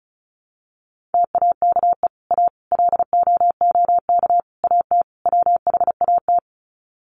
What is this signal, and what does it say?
Morse code, heard as a keyed tone, about 34 wpm, 700 Hz. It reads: TAKE A LOOK AT WHAT
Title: Sets of 5 Words Encore2 35wpm